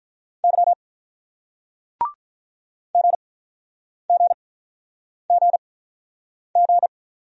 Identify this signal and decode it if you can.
Morse code, heard as a keyed tone, about 46 wpm, spaced as for 9 wpm, 700 Hz. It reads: Y GGGG